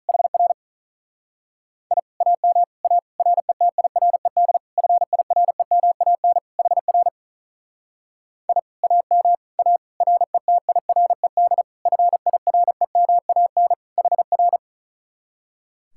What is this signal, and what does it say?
Morse code, heard as a keyed tone, about 44 wpm, 700 Hz. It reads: HR I AM A RETIRED FIREMAN HR I AM A RETIRED FIREMAN HR
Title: Speed-Racing Common QSO Phrases Encore - 35wpm